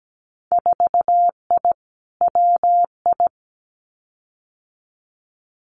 Morse code, heard as a keyed tone, about 17 wpm, 700 Hz. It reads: 4I WI
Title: Winter Field Day Exchanges 17wpm